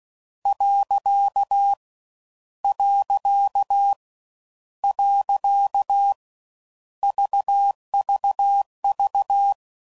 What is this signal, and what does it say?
Morse code, heard as a keyed tone, about 16 wpm, 800 Hz. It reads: . . VVV